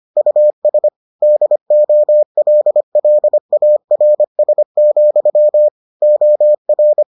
Morse code, heard as a keyed tone, about 25 wpm, 600 Hz. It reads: US DOLLARS, OR